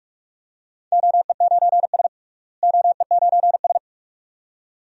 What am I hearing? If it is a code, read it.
Morse code, heard as a keyed tone, about 46 wpm, 700 Hz. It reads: OE9S OE9S
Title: DXCC-100 Call Signs - 4 Characters - Repeated 2x - 45wpm